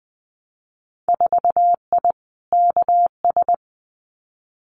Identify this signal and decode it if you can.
Morse code, heard as a keyed tone, about 19 wpm, 700 Hz. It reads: 4I KS
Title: Winter Field Day Exchanges Encore 20wpm